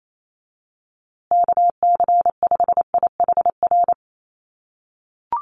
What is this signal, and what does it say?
Morse code, heard as a keyed tone, about 28 wpm, 700 Hz. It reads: KC5IHR
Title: USA Call Signs Repeated 3x 28wpm